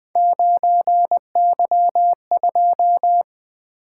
Morse code, heard as a keyed tone, about 20 wpm, 700 Hz. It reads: 9Y2
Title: Three Letter-Number Combinations - 20wpm